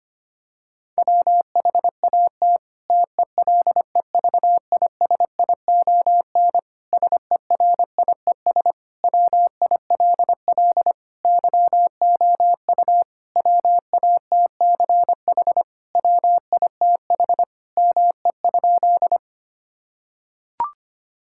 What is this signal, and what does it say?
Morse code, heard as a keyed tone, about 25 wpm, 700 Hz. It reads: WHAT TELEVISION SERIES WILL YOU WATCH WITH ME?
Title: Sentences from Top 1000 Words - Review Part 5 - 25wpm